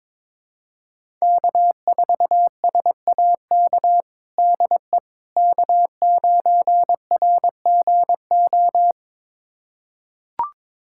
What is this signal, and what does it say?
Morse code, heard as a keyed tone, about 22 wpm, 700 Hz. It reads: K4SAK DE K9RGO
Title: USA Callsign DE Callsign - 5 Characters - 22wpm